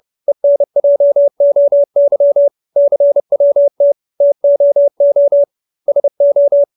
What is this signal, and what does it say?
Morse code, heard as a keyed tone, about 30 wpm, 570 Hz. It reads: ENJOY CWT TOO SO